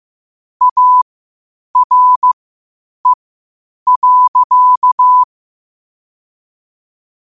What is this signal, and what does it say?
Morse code, heard as a keyed tone, about 15 wpm, spaced as for 8 wpm, 1000 Hz. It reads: ARE.
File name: qotdfiles / 8wpm